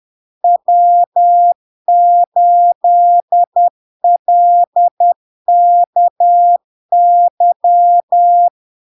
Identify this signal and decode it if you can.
Morse code, heard as a keyed tone, about 10 wpm, 700 Hz. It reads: W8LKY
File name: LARK-FD-2022-Log-in-CW-10wpm-700-Hz